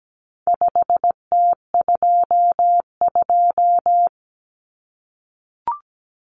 Morse code, heard as a keyed tone, about 17 wpm, 700 Hz. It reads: 5T22
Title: Common Cut Numbers - 4-Digit 17wpm